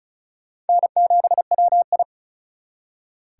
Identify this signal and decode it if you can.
Morse code, heard as a keyed tone, about 42 wpm, 700 Hz. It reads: N7WI
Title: Speed Racing - USA 1x2 Call Signs 35wpm